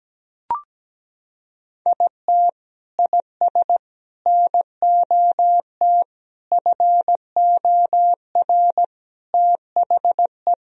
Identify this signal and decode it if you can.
Morse code, heard as a keyed tone, about 17 wpm, 700 Hz. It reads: IT IS NOT FOR THE
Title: Sentences from Top 1000 Words - Review Part 1 - 17wpm